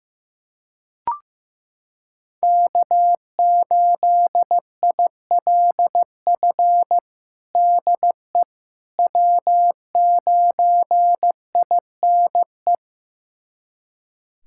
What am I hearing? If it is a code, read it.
Morse code, heard as a keyed tone, about 16 wpm, 700 Hz. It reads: K8ILF DE W9INE
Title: USA Callsign DE Callsign - 5 Characters - 15wpm